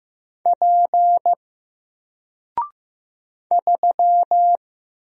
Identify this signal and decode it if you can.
Morse code, heard as a keyed tone, about 15 wpm, 700 Hz. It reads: P 3
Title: Single Letter-Number 15wpm